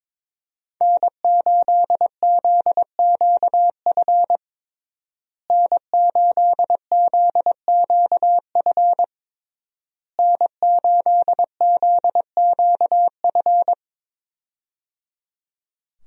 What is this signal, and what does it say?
Morse code, heard as a keyed tone, about 22 wpm, 700 Hz. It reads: N8ZQF N8ZQF N8ZQF
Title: USA Call Signs - 5 Characters - Repeated 3x - 22wpm